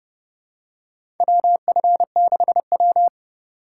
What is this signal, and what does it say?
Morse code, heard as a keyed tone, about 30 wpm, 700 Hz. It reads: WF6W
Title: USA Call Signs - 4x Character Spacing - 30wpm